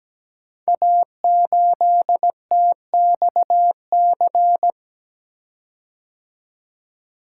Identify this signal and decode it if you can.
Morse code, heard as a keyed tone, about 17 wpm, 690 Hz. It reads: A8TXC
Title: Code Groups - 5 Letter:Numbers - 17wpm